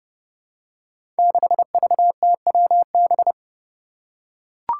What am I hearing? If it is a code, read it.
Morse code, heard as a keyed tone, about 31 wpm, 700 Hz. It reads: 6VTWB
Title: Five Challenging Letter-Number Combinations 30wpm